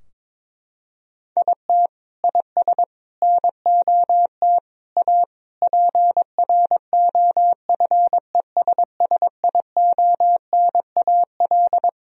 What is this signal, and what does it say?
Morse code, heard as a keyed tone, about 22 wpm, 700 Hz. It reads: IT IS NOT A PROFESSIONAL
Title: Sentences from Top 1000 Words - Review Part 4 - 22wpm